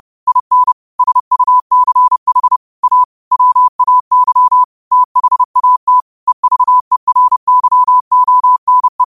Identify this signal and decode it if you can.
Morse code, heard as a keyed tone, about 30 wpm, 1000 Hz. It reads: IN SUCH A WAY THAT EVERYONE
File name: qotdfiles / 30wpm